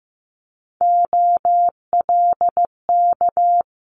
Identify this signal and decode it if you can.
Morse code, heard as a keyed tone, about 15 wpm, 700 Hz. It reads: OLK
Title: Multi-letter Phonemes Quick Fire 15wpm